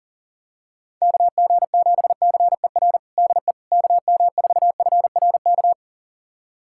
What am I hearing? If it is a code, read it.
Morse code, heard as a keyed tone, about 40 wpm, 700 Hz. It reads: KG7CER DE KM4FRK
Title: USA Callsign DE Callsign 40wpm